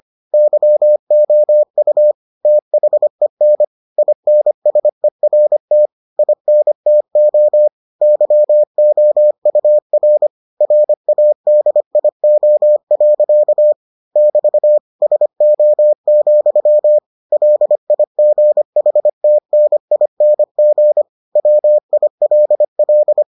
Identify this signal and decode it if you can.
Morse code, heard as a keyed tone, about 25 wpm, 600 Hz. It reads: YOU THEN INSERT INTO YOUR RADIO. <BT> SO, LIGHTNING WILL